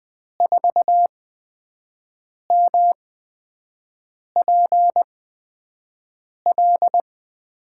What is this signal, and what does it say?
Morse code, heard as a keyed tone, about 20 wpm, spaced as for 5 wpm, 700 Hz. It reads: 4MPL
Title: USA Call Signs - 6 Characters - 8x Character Spacing - 20wpm